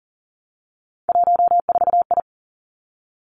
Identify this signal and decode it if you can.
Morse code, heard as a keyed tone, about 40 wpm, 700 Hz. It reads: 14I
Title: Three Challenging Letter-Number Combinations 40wpm